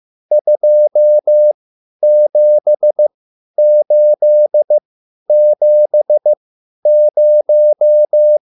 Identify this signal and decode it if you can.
Morse code, heard as a keyed tone, about 15 wpm, spaced as for 10 wpm, 600 Hz. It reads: 27870
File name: QSL / CW lesson